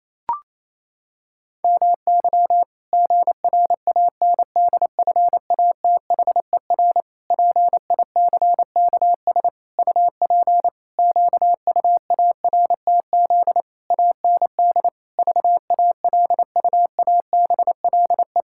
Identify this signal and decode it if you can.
Morse code, heard as a keyed tone, about 28 wpm, 700 Hz. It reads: MY GRANDFATHER PICKS UP QUARTZ AND VALUABLE
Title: Pangram Challenge v2 - 28wpm